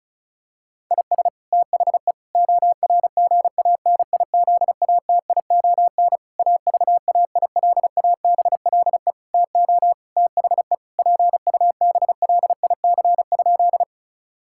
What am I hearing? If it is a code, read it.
Morse code, heard as a keyed tone, about 36 wpm, 700 Hz. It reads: IS THE ORGANIZATION AVAILABLE TO THE PUBLIC?
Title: Sentences from Top 1000 Words - Review Part 5 - 35wpm